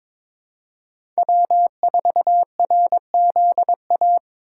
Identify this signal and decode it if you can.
Morse code, heard as a keyed tone, about 23 wpm, 700 Hz. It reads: W4RZA
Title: USA Call Signs - 5 Characters - 8x Character Spacing - 22wpm